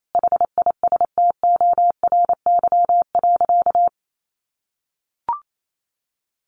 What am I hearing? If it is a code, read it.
Morse code, heard as a keyed tone, about 28 wpm, 700 Hz. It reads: HISTORY.
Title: Sentences from Top 1300 Words 28wpm